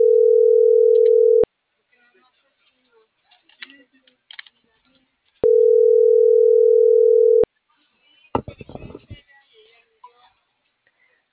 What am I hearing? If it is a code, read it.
Ringback tone (North America), 440 + 480 Hz.